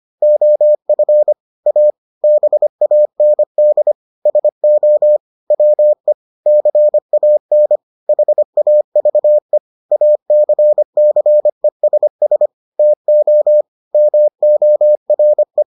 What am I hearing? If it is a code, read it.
Morse code, heard as a keyed tone, about 25 wpm, 600 Hz. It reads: OF A BAND SO WE CAN HAVE ACCESS TO MORE